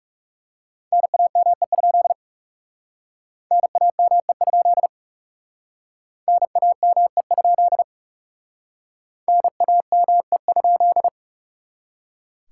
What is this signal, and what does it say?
Morse code, heard as a keyed tone, about 38 wpm, 700 Hz. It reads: NAME? NAME? NAME? NAME?